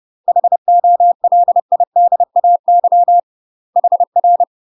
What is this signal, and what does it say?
Morse code, heard as a keyed tone, about 30 wpm, 700 Hz. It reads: HOLIDAY HR
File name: UK MASTER QSOs TRAINING / 30WPM / UK MASTER QSOs 0034